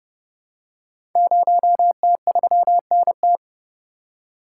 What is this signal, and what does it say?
Morse code, heard as a keyed tone, about 29 wpm, 700 Hz. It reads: 0T3NT
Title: Code Groups - 5 Letter:Numbers - 30wpm